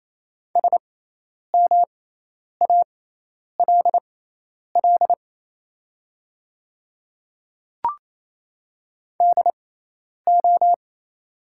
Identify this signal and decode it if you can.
Morse code, heard as a keyed tone, about 28 wpm, spaced as for 10 wpm, 700 Hz. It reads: SMALL DO
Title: 6x Character Spacing - Top 200 Words - 28wpm